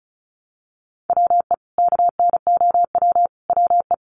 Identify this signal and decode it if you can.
Morse code, heard as a keyed tone, about 34 wpm, 700 Hz. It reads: WE KNOW WE
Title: Sentences from Top 1100 Words 35wpm